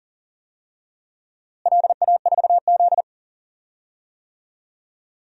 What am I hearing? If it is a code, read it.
Morse code, heard as a keyed tone, about 38 wpm, 700 Hz. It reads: LA4Z
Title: DXCC-100 Call Signs - 4 Characters - Repeated 2x - 40wpm